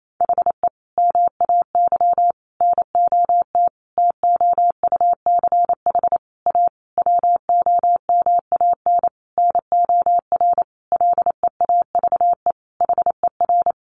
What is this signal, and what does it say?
Morse code, heard as a keyed tone, about 28 wpm, 700 Hz. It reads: HE MAY NOT TOUCH A WOMAN NOR LEAVE HER
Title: Sentences from Top 1500 Words 28wpm